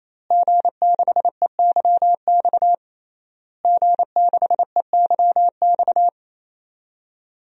Morse code, heard as a keyed tone, about 28 wpm, 700 Hz. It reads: G6EYX G6EYX
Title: DXCC-100 Call Signs - 5 Characters - Repeated 3x - 28wpm